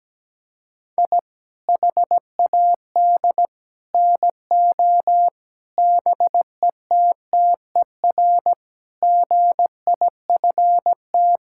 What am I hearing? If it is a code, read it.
Morse code, heard as a keyed tone, about 17 wpm, 700 Hz. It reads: I HAD NO BETTER GIFT
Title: Sentences from Top 1000 Words - Review Part 6 - 17wpm